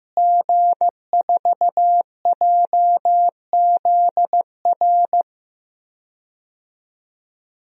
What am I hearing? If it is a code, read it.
Morse code, heard as a keyed tone, about 16 wpm, 700 Hz. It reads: G4JZR
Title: DXCC-100 Call Signs - 5 Characters - Repeated 3x - 15wpm